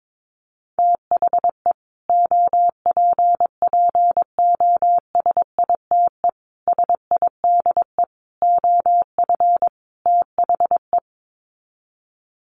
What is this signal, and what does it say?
Morse code, heard as a keyed tone, about 22 wpm, 700 Hz. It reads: THE OPPOSITE SIDE OF THE
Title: Sets of 5 Words Encore 22wpm